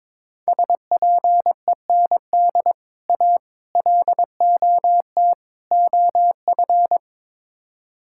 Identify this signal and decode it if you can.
Morse code, heard as a keyed tone, about 22 wpm, 700 Hz. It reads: SPEND A LOT OF
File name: practice / 22wpm / Build A Phrase - 5-Word Phrases - 22wpm